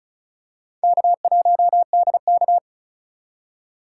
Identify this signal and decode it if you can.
Morse code, heard as a keyed tone, about 34 wpm, 700 Hz. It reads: K1DK
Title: USA Call Signs - 4 Characters - 6x Character Spacing - 35wpm